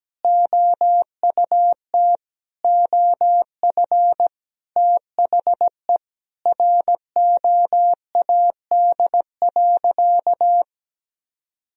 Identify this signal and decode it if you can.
Morse code, heard as a keyed tone, about 17 wpm, 700 Hz. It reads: OUT OF THE ROAD.
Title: Sentences from Top 400 Words and Abbreviations 17wpm